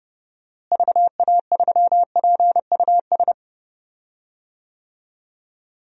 Morse code, heard as a keyed tone, about 30 wpm, 700 Hz. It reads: VA3PUS
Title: DXCC-100 Call Signs - 6 Characters - Repeated 2x - 30wpm